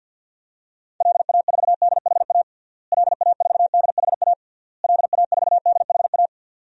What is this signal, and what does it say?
Morse code, heard as a keyed tone, about 48 wpm, 700 Hz. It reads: LA4DHA LA4DHA LA4DHA
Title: DXCC-100 Call Signs - 6 Characters - Repeated 3x - 50wpm